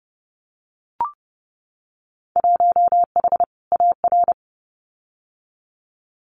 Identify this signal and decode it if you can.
Morse code, heard as a keyed tone, about 30 wpm, 700 Hz. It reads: 1H AR
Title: Winter Field Day Exchanges 30wpm